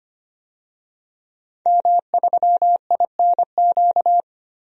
Morse code, heard as a keyed tone, about 25 wpm, 700 Hz. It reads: M3INQ
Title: DXCC-100 Call Signs - 5 Characters - 25wpm